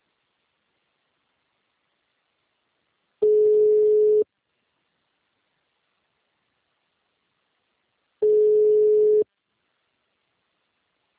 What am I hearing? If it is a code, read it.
Ringback tone (Europe, 425 Hz).